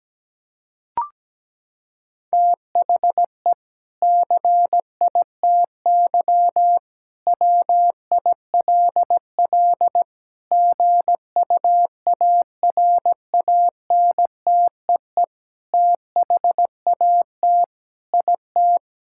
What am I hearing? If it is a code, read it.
Morse code, heard as a keyed tone, about 17 wpm, 700 Hz. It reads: THE CITY WILL GUARANTEE THAT IT
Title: Sentences from Top 1600 Words 17wpm